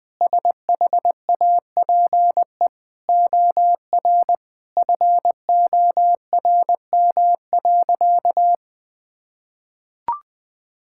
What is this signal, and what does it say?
Morse code, heard as a keyed tone, about 20 wpm, 700 Hz. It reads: SHAPE OR FORM.
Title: Sentences with 6-Word Phrases - 20wpm